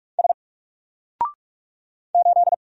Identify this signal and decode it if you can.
Morse code, heard as a keyed tone, about 45 wpm, 700 Hz. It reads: S 8
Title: Single Character (Letter:Number:Punctuation:Prosign) Alt 45wpm